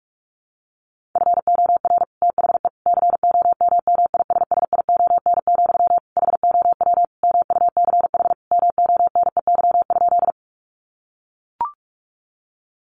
Morse code, heard as a keyed tone, about 44 wpm, 700 Hz. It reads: FOR THE COMMISSION, HOW MUCH MONEY?
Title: Sentences from Top 1400 Words 45wpm